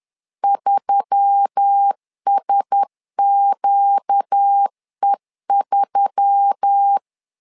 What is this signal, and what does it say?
Morse code, heard as a keyed tone, about 10 wpm, 800 Hz. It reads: 3SQE3
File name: LEZIONI / 4 Q1S3 10wpm 2427